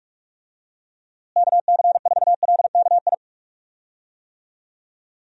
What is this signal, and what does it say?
Morse code, heard as a keyed tone, about 43 wpm, 700 Hz. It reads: KC4LKI
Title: USA Call Signs - 6 Characters - Repeated 3x - 45wpm